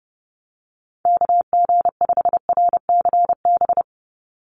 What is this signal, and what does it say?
Morse code, heard as a keyed tone, about 30 wpm, 700 Hz. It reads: KG5RCB
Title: USA Call Signs 30wpm